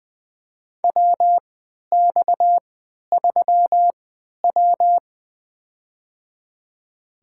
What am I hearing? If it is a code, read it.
Morse code, heard as a keyed tone, about 20 wpm, spaced as for 11 wpm, 700 Hz. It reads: WX3W
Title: USA Call Signs - 4 Characters - 3x Character Spacing - 20wpm